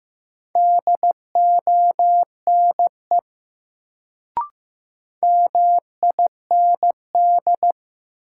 Morse code, heard as a keyed tone, about 15 wpm, 700 Hz. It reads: DONE MIND
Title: Sentences from Top 500 Words - New Word Drill - 15wpm